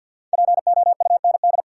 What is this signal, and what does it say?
Morse code, heard as a keyed tone, about 52 wpm, 700 Hz. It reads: POUND
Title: Sentences from Top 1000 Words - Review Part 2 - New Word Drill - 50wpm